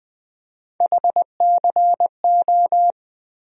Morse code, heard as a keyed tone, about 20 wpm, 700 Hz. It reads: HCO
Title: USA Call Sign Postfixes - 3 Letters 20wpm